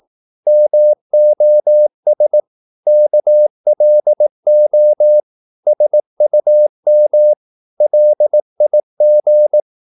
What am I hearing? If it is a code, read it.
Morse code, heard as a keyed tone, about 18 wpm, 600 Hz. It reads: MOS KLO SUM LIG
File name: Korte ord DK 18wpm